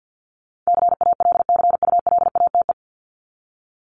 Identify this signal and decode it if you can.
Morse code, heard as a keyed tone, about 50 wpm, 700 Hz. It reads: CALCULATE
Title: Top 1751-2000 Words 50wpm